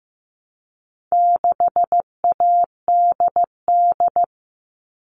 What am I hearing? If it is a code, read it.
Morse code, heard as a keyed tone, about 15 wpm, 700 Hz. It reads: 6ADD
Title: All Cut Numbers - 4-Digit 15wpm